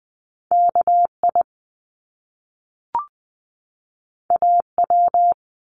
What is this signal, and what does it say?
Morse code, heard as a keyed tone, about 20 wpm, 700 Hz. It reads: KI AW
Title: Top Two Letter Combinations in Words 20wpm